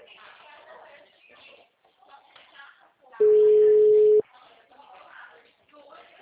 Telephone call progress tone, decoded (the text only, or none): ringback tone